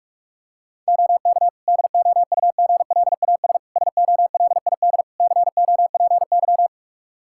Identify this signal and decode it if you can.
Morse code, heard as a keyed tone, about 46 wpm, 700 Hz. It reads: OK DOUGLAS SOLID COPY
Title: Practice QSO - Exchange 3 - More Info - 45wpm